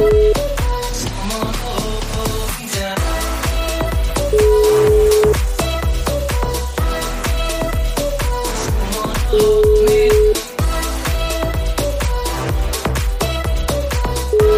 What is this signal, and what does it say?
Ringback tone (Europe, 425 Hz), with noise.